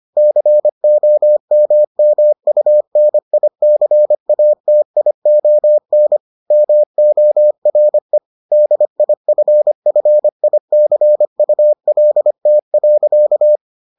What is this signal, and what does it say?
Morse code, heard as a keyed tone, about 25 wpm, 600 Hz. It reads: COMMUNICATION MORE DIFFICULT.